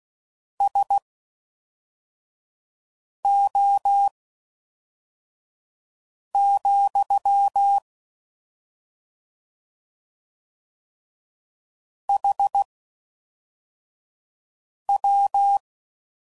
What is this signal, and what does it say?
Morse code, heard as a keyed tone, about 16 wpm, spaced as for 4 wpm, 800 Hz. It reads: SO, HW